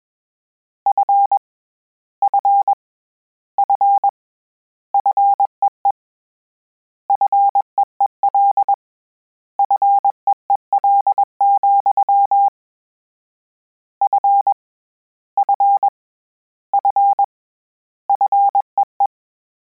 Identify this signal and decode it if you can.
Morse code, heard as a keyed tone, about 21 wpm, 800 Hz. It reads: F F F FEE FEEL FEEL, F F F FEE
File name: Combinations / Combinations-mp3s / Combinations@20